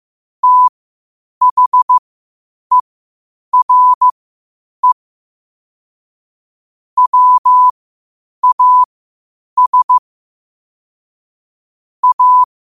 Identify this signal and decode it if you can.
Morse code, heard as a keyed tone, about 15 wpm, spaced as for 8 wpm, 1000 Hz. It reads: THERE WAS A
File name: qotdfiles / 8wpm